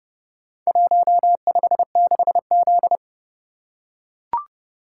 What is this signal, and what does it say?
Morse code, heard as a keyed tone, about 29 wpm, 700 Hz. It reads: 156Z